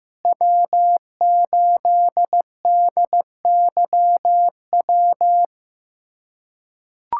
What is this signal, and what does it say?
Morse code, heard as a keyed tone, about 15 wpm, 700 Hz. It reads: W8DYW
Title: USA Callsign DE Callsign - 5 Characters - 15wpm